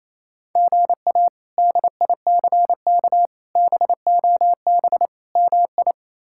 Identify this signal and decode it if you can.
Morse code, heard as a keyed tone, about 28 wpm, 700 Hz. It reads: GA DICK BOB MI
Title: K1USN SST QSO Practice - 28wpm